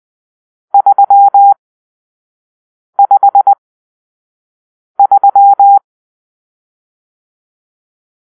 Morse code, heard as a keyed tone, about 20 wpm, spaced as for 5 wpm, 800 Hz. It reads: 353
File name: arrl / Grp-6 Drill-4